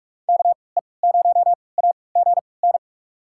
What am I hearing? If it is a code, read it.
Morse code, heard as a keyed tone, about 42 wpm, spaced as for 25 wpm, 690 Hz. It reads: KE0AGN
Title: USA Call Signs - 6 Characters - 3x Character Spacing - 45wpm